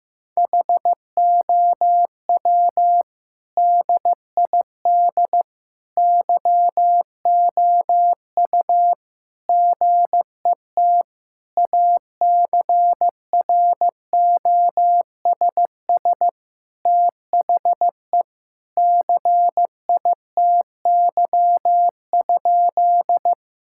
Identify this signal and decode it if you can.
Morse code, heard as a keyed tone, about 15 wpm, 700 Hz. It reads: HOW DID YOU GET ACROSS THE CITY?
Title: Sentences from Top 1000 Words - Review Part 5 - 15wpm